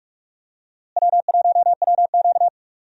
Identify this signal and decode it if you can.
Morse code, heard as a keyed tone, about 45 wpm, 700 Hz. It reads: W1WQ
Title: USA Call Signs - 4 Characters - Repeated 2x - 45wpm